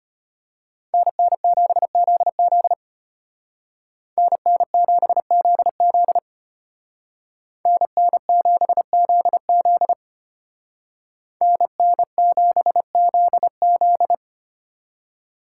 Morse code, heard as a keyed tone, about 33 wpm, 700 Hz. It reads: NN7ZZ NN7ZZ NN7ZZ NN7ZZ